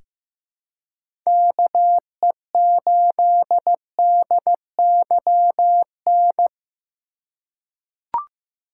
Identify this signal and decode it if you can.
Morse code, heard as a keyed tone, about 15 wpm, 700 Hz. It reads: KE8DYN
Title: USA Call Signs - 6 Characters - 6x Character Spacing - 15wpm